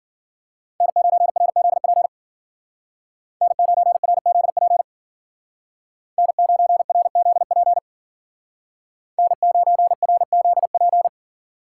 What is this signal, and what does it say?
Morse code, heard as a keyed tone, about 50 wpm, 700 Hz. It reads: N9RZP N9RZP N9RZP N9RZP
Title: Speed Racing - USA 1x3 Call Signs 40wpm